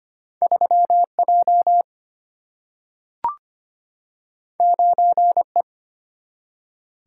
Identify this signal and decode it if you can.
Morse code, heard as a keyed tone, about 25 wpm, 700 Hz. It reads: 3J 9E
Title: Single-Letter Single-Number 25wpm